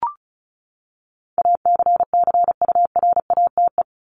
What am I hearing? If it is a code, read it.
Morse code, heard as a keyed tone, about 34 wpm, 700 Hz. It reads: ACCURATE
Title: Top 1001-1250 Words 35wpm